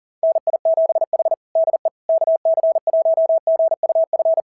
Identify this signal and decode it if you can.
Morse code, heard as a keyed tone, about 40 wpm, 690 Hz. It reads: NI7H DE KC1GUF
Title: SOTA QSO Practice 40wpm